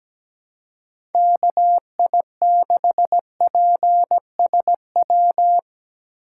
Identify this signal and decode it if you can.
Morse code, heard as a keyed tone, about 17 wpm, 700 Hz. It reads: KI6PSW
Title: USA Call Signs - 3x Character Spacing - 17wpm